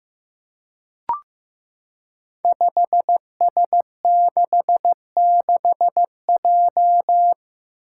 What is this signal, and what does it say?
Morse code, heard as a keyed tone, about 15 wpm, 700 Hz. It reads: 5S66J
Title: Five Challenging Letter-Number Combinations 15wpm